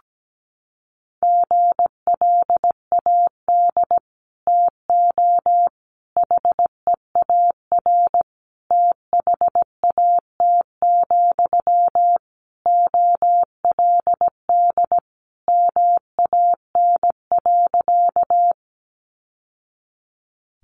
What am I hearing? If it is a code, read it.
Morse code, heard as a keyed tone, about 17 wpm, 700 Hz. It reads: GLAD TO HEAR THAT, OLD MAN.
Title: Sentences from Top 800 Words 17wpm